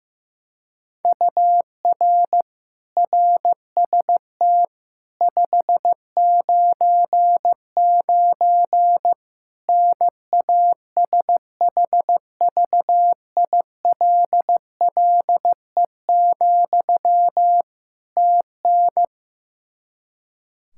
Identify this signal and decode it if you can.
Morse code, heard as a keyed tone, about 16 wpm, 700 Hz. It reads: UR RST 599 NASHVILLE, TN
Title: Practice QSO - Exchange 1 - Beginning - 15wpm